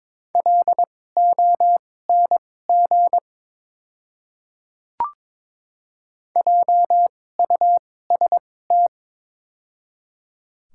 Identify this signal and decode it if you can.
Morse code, heard as a keyed tone, about 22 wpm, spaced as for 16 wpm, 700 Hz. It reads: LONG JUST
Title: 2x Character Spacing - Top 200 Words - 22wpm